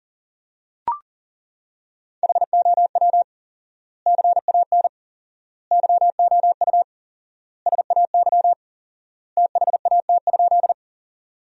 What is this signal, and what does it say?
Morse code, heard as a keyed tone, about 37 wpm, 700 Hz. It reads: HOW CAN YOU SAY THAT?
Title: 4x Word-Spacing - Sentences from Top 200 Words - 40wpm